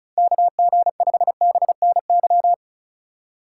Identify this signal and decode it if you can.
Morse code, heard as a keyed tone, about 45 wpm, 700 Hz. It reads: KG5BNY